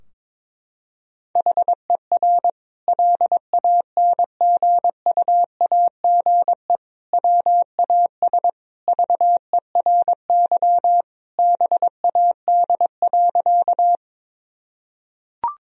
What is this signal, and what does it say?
Morse code, heard as a keyed tone, about 21 wpm, 700 Hz. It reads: HER LANGUAGE WAS VERY BAD.
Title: Sentences from Top 1000 Words - Review Part 3 - 22wpm